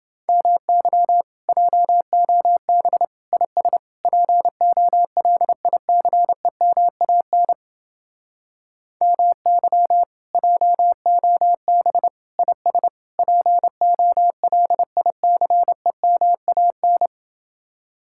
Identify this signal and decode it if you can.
Morse code, heard as a keyed tone, about 29 wpm, 700 Hz. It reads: MY JOB IS POLICEMAN MY JOB IS POLICEMAN